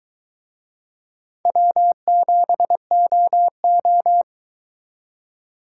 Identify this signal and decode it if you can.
Morse code, heard as a keyed tone, about 22 wpm, 700 Hz. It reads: W7OO